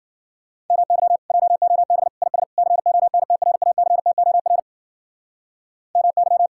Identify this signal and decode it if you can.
Morse code, heard as a keyed tone, about 52 wpm, 700 Hz. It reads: MY JOB IS CONTRACTOR MY